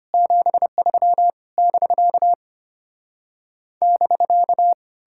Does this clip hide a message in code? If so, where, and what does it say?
Yes: Morse code from 0.1 to 4.7 s, heard as a keyed tone, about 33 wpm, 700 Hz. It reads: 73 <BK> <BK>